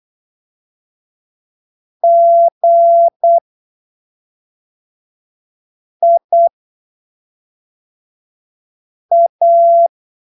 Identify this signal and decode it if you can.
Morse code, heard as a keyed tone, about 8 wpm, spaced as for 3 wpm, 680 Hz. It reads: GIA
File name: texto 24 08ppm